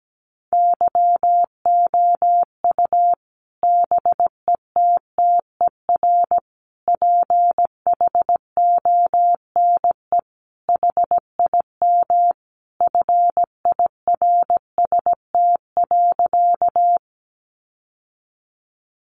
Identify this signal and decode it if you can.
Morse code, heard as a keyed tone, about 17 wpm, 700 Hz. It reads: YOU BETTER PHONE HIM FIRST.
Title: Sentences from Top 1100 Words 17wpm